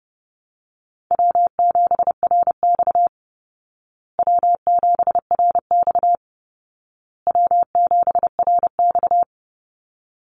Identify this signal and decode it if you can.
Morse code, heard as a keyed tone, about 30 wpm, 700 Hz. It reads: W7RX W7RX W7RX
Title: Easy Call Signs Repeated 3x - 30wpm